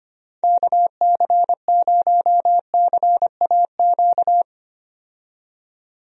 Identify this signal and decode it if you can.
Morse code, heard as a keyed tone, about 25 wpm, 700 Hz. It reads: KC0CAQ
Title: USA Call Signs - 6 Characters - 25wpm